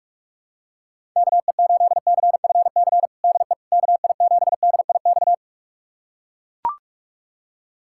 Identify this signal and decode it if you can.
Morse code, heard as a keyed tone, about 41 wpm, 700 Hz. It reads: KE8CFC DE KI7DIX
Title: USA Callsign DE Callsign - 6 Characters - 45wpm